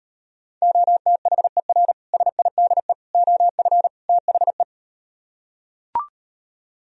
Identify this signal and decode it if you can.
Morse code, heard as a keyed tone, about 38 wpm, 700 Hz. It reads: OTHER SIDE OF THE